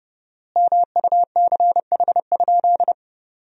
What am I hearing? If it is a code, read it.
Morse code, heard as a keyed tone, about 32 wpm, 700 Hz. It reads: MUCH?
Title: 4x Word-Spacing - Sentences from Top 200 Words - 30wpm